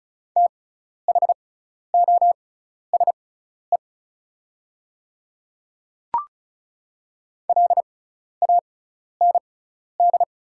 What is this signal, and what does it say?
Morse code, heard as a keyed tone, about 36 wpm, spaced as for 12 wpm, 690 Hz. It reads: THOSE LAND
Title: 6x Character Spacing - Top 300 Words - 35wpm